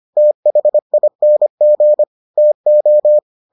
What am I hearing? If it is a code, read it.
Morse code, heard as a keyed tone, about 25 wpm, 600 Hz. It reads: THING TO